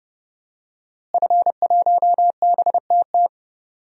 Morse code, heard as a keyed tone, about 29 wpm, 700 Hz. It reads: F1BTT
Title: DXCC-100 Call Signs - 5 Characters - Repeated 3x - 30wpm